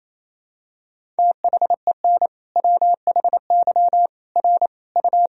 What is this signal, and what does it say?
Morse code, heard as a keyed tone, about 28 wpm, 700 Hz. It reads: THEN WHY R U
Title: Sentences from Top 200 Words and Abbreviations 28wpm